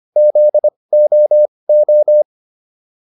Morse code, heard as a keyed tone, about 25 wpm, spaced as for 20 wpm, 600 Hz. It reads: ZOO